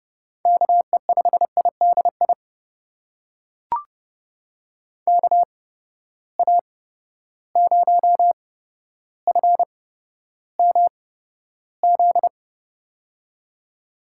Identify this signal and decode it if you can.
Morse code, heard as a keyed tone, about 30 wpm, spaced as for 13 wpm, 700 Hz. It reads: KE5IDI KA0FMZ
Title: USA Call Signs - 6 Characters - 8x Character Spacing - 30wpm